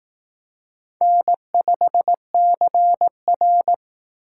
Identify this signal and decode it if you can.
Morse code, heard as a keyed tone, about 18 wpm, 700 Hz. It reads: N5CR
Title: Speed Racing - USA Call Signs 15wpm